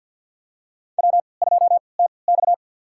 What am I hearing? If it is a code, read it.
Morse code, heard as a keyed tone, about 51 wpm, spaced as for 28 wpm, 700 Hz. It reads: W2TX
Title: USA Call Signs - 3x Character Spacing - 50wpm